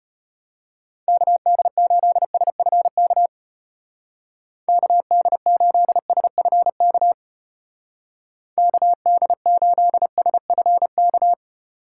Morse code, heard as a keyed tone, about 30 wpm, 700 Hz. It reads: KD8SFK KD8SFK KD8SFK
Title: Speed Racing - USA 2x3 Call Signs 25wpm